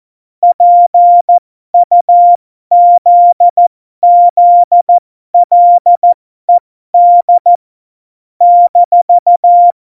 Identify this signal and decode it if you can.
Morse code, heard as a keyed tone, about 14 wpm, spaced as for 12 wpm, 700 Hz. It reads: PUZZLED -
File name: Morsekurs / Witze